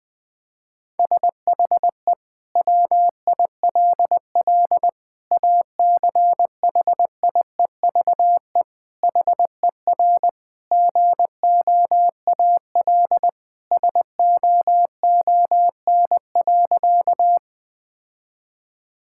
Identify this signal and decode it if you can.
Morse code, heard as a keyed tone, about 20 wpm, 700 Hz. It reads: SHE WILL ACHIEVE HER GOAL SOON.
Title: Sentences from Top 1000 Words - Review Part 6 - 20wpm